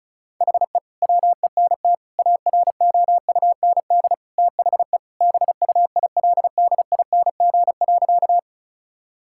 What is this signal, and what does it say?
Morse code, heard as a keyed tone, about 36 wpm, 700 Hz. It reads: HE WENT AROUND THE BUILDING.
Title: Sentences from Top 1000 Words - Review Part 5 - 35wpm